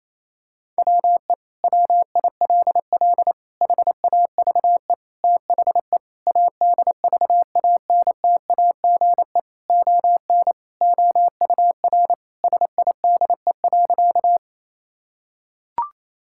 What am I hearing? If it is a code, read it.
Morse code, heard as a keyed tone, about 28 wpm, 700 Hz. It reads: WE WILL HAVE THE ADVANTAGE ON OUR SIDE.
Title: Sentences from Top 1000 Words - Review Part 6 - 28wpm